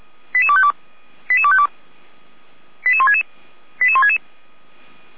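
Five-tone selective call (ZVEI), heard as quick runs of five tones, tones of about 70 ms. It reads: 88252 88252 88166 88166
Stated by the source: ZVEI five-tone selective call received by SDR, from audio.f85360.feuerwehr-orlamuende.org, a fire brigade (Feuerwehr Orlamünde) server.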